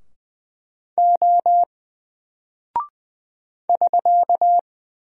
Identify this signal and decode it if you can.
Morse code, heard as a keyed tone, about 20 wpm, 700 Hz. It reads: O <SK>
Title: Single Character (Letter:Number:Punctuation:Prosign) Alt 20wpm